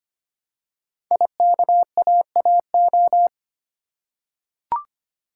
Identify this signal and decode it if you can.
Morse code, heard as a keyed tone, about 25 wpm, 700 Hz. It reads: IKAAO
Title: Code Groups - 5 Letters - 25wpm